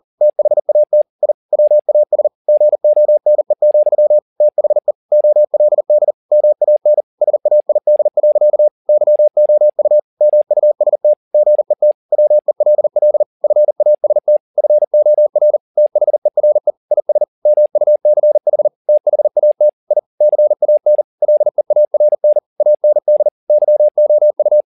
Morse code, heard as a keyed tone, about 40 wpm, 600 Hz. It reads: THAT I WAS GONE, THE OLD MAN SAID. YOU MUST GET WELL FAST FOR THERE IS MUCH THAT I CAN LEARN AND YOU
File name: the old man and the sea 2 40wpm